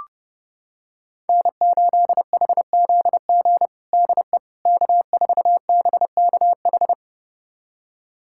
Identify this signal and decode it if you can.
Morse code, heard as a keyed tone, about 30 wpm, 700 Hz. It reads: N8HZG DE K4BKH
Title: USA Callsign DE Callsign 30wpm